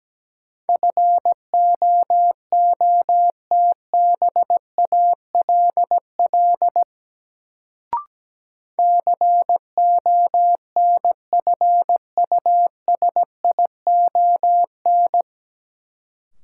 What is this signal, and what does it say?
Morse code, heard as a keyed tone, about 17 wpm, 700 Hz. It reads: FOOTBALL CONFUSION
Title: Sentences from Top 1600 Words - New Word Drill Alt - 17wpm